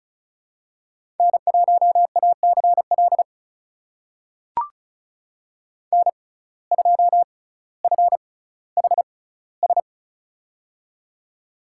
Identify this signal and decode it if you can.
Morse code, heard as a keyed tone, about 34 wpm, spaced as for 18 wpm, 700 Hz. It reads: N1ACL N2FHS
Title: USA Call Signs - 5 Characters - 6x Character Spacing - 35wpm